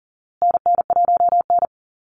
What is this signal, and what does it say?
Morse code, heard as a keyed tone, about 40 wpm, 700 Hz. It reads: NN1N
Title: Call Signs Repeated 3x - 40wpm